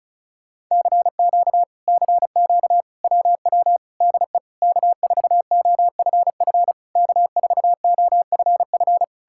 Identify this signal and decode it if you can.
Morse code, heard as a keyed tone, about 40 wpm, 700 Hz. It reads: CQ CQ WW DE K4OFF K4OFF